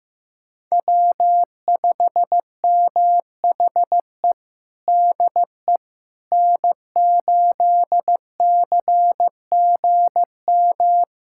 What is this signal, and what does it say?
Morse code, heard as a keyed tone, about 15 wpm, 700 Hz. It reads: W5MHE DE N8CGM
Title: USA Callsign DE Callsign - 5 Characters - 15wpm